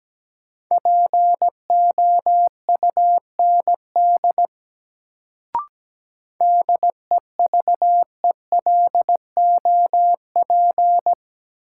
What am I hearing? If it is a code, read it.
Morse code, heard as a keyed tone, about 17 wpm, 700 Hz. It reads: POUND DEVELOP
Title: Sentences from Top 1000 Words - Review Part 2 - New Word Drill - 17wpm